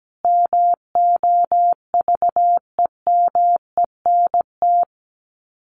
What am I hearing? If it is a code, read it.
Morse code, heard as a keyed tone, about 17 wpm, 700 Hz. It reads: MOVEMENT
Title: Sets of 5 Words Encore2 17wpm